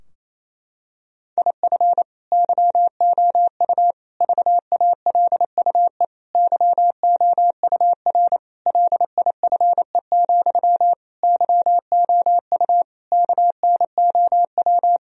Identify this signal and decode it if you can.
Morse code, heard as a keyed tone, about 28 wpm, 700 Hz. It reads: IF YOU VALUE YOUR LIFE, YOU KNOW
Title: Sentences from Top 1000 Words - Review Part 3 - 28wpm